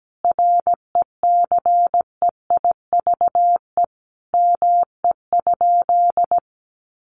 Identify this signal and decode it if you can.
Morse code, heard as a keyed tone, about 17 wpm, 700 Hz. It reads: RECEIVE ME?
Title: Sentences from Top 1000 Words 17wpm